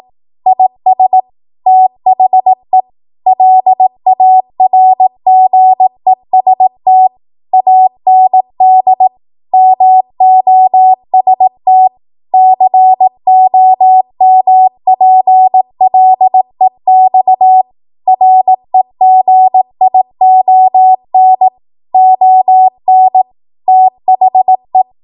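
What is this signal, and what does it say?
Morse code, heard as a keyed tone, about 18 wpm, 750 Hz. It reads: IS THE LARGEST AND MOST COMPLEX REGION ON THE
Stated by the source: Code practice at 18 WPM, a transition run bridging 15 to 20 WPM.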